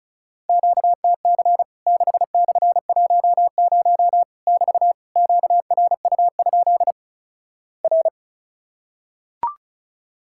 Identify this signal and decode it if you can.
Morse code, heard as a keyed tone, about 36 wpm, 700 Hz. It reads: QTC 6/10 <BT> QRU? R
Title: Worked All Europe DX Contest - QTC Exchange - 35wpm